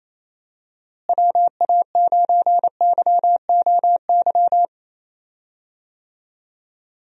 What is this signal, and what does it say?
Morse code, heard as a keyed tone, about 28 wpm, 700 Hz. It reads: WA9YOY
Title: USA Call Signs - 6 Characters - Repeated 2x - 28wpm